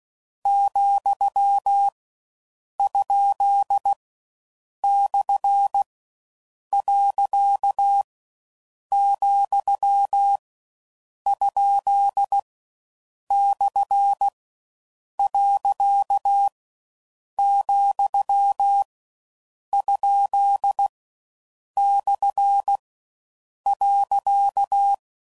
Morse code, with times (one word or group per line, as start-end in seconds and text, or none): 0.5-1.9 ,
2.8-3.9 ?
4.8-5.8 /
6.7-8.0 .
8.9-10.4 ,
11.3-12.4 ?
13.3-14.3 /
15.2-16.5 .
17.4-18.8 ,
19.7-20.9 ?
21.8-22.8 /
23.7-24.9 .